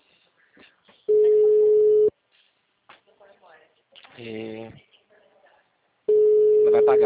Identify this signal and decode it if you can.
Ringback tone (Europe, 425 Hz), with speech.